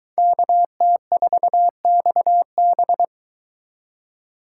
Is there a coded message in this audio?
Yes: Morse code from 0.2 to 3.0 s, heard as a keyed tone, about 22 wpm, 700 Hz. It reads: KT4XB